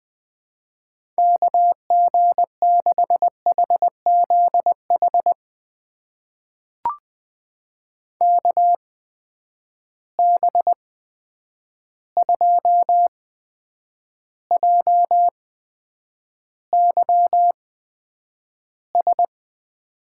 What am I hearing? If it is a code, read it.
Morse code, heard as a keyed tone, about 20 wpm, spaced as for 9 wpm, 700 Hz. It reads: KG6HZH KB2JYS
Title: USA Call Signs - 6 Characters - 8x Character Spacing - 20wpm